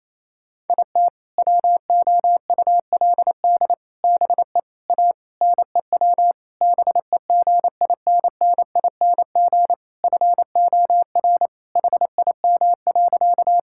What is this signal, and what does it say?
Morse code, heard as a keyed tone, about 28 wpm, 700 Hz. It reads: IT WOULD BE A NEW BEGINNING FOR HIM.
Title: Sentences from Top 1000 Words - Review Part 5 - 28wpm